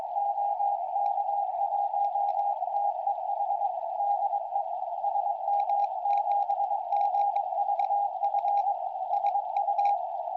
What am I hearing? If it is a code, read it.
Morse code, heard as a keyed tone, about 23 wpm, 810 Hz. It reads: OK1TEH DE SM7GVF SU
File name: tropo / sm7gvf 70cm